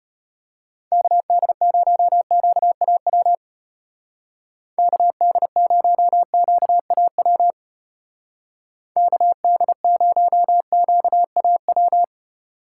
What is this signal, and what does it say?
Morse code, heard as a keyed tone, about 31 wpm, 700 Hz. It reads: KD0QAW KD0QAW KD0QAW